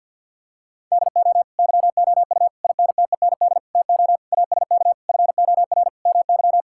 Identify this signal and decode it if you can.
Morse code, heard as a keyed tone, about 50 wpm, 700 Hz. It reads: DO YOU INTEND TO ASK FOR MY
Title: Sentences from Top 1600 Words 50wpm